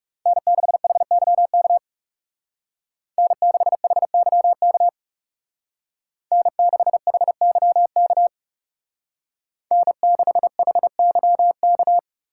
Morse code, heard as a keyed tone, about 39 wpm, 700 Hz. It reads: N6HYK N6HYK N6HYK N6HYK